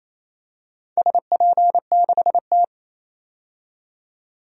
Morse code, heard as a keyed tone, about 27 wpm, 700 Hz. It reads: SP6T
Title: DXCC-100 Call Signs - 4 Characters - 28wpm